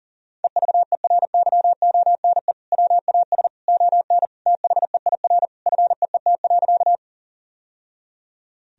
Morse code, heard as a keyed tone, about 40 wpm, 700 Hz. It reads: EVERYONE WAS ON THEIR FEET.
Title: Sentences from Top 1000 Words - Review Part 1 - 40wpm